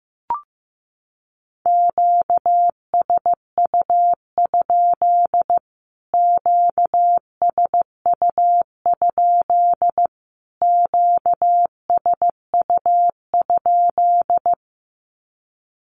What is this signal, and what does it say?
Morse code, heard as a keyed tone, about 15 wpm, 700 Hz. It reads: QSU? QSU? QSU?
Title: Q-Codes Repeated 3x 15wpm